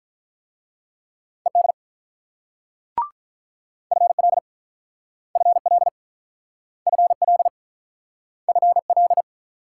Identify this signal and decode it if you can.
Morse code, heard as a keyed tone, about 46 wpm, 700 Hz. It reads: ED FL FL FL FL